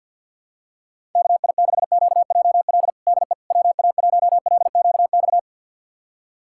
Morse code, heard as a keyed tone, about 47 wpm, 700 Hz. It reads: KI6QJL DE WA1LQX
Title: USA Callsign DE Callsign - 6 Characters - 50wpm